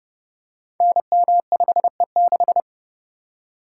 Morse code, heard as a keyed tone, about 30 wpm, 700 Hz. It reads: NM5E6
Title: Code Groups - 5 Letter:Numbers - 30wpm